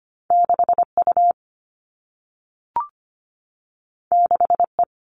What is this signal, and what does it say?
Morse code, heard as a keyed tone, about 25 wpm, 700 Hz. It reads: 6 U 6 E
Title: All Cut Numbers - 2-Digit 25wpm